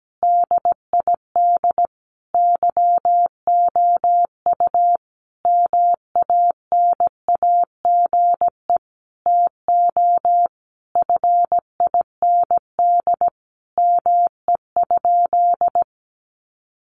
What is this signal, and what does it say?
Morse code, heard as a keyed tone, about 17 wpm, 700 Hz. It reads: DID YOU MANAGE TO FIND ME?
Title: Sentences from Top 1400 Words 17wpm